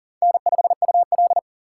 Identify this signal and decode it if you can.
Morse code, heard as a keyed tone, about 50 wpm, 700 Hz. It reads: N5UL
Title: Speed Racing - USA 1x2 Call Signs 40wpm